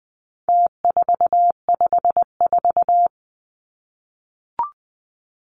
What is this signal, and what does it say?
Morse code, heard as a keyed tone, about 20 wpm, 700 Hz. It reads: T454
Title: Numbers - Contest Style - 4-Digit 20wpm